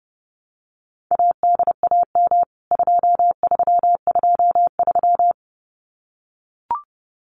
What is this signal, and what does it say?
Morse code, heard as a keyed tone, about 30 wpm, 700 Hz. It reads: ADAM 2323
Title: CWT Exchange 30wpm